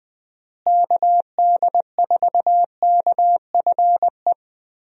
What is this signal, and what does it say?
Morse code, heard as a keyed tone, about 20 wpm, 700 Hz. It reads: KD4KFE
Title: USA Call Signs - 6 Characters - Repeated 3x - 20wpm